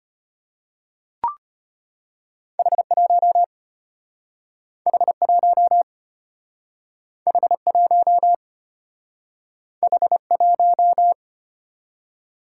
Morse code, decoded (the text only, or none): H1 H1 H1 H1